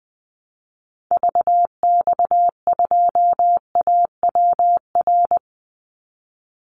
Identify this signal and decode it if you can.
Morse code, heard as a keyed tone, about 20 wpm, 700 Hz. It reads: VX2AWR
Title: Call Signs Repeated 2x - 20wpm